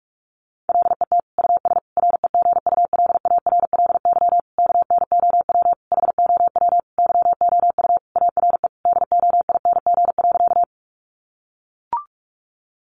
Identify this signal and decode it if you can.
Morse code, heard as a keyed tone, about 44 wpm, 700 Hz. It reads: LET US REGULARLY KNOW HOW YOU ARE DOING.
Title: Sentences from Top 1400 Words 45wpm